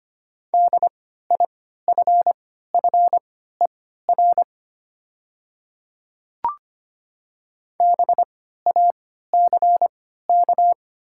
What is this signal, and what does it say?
Morse code, heard as a keyed tone, about 25 wpm, spaced as for 14 wpm, 700 Hz. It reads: DIFFER BACK
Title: 3x Character Spacing - Top 200 Words - 25wpm